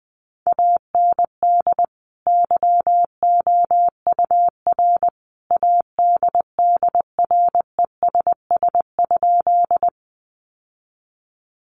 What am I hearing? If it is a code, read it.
Morse code, heard as a keyed tone, about 20 wpm, 700 Hz. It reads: AND YOUR ADDRESS?
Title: Sentences from Top 1000 Words 20wpm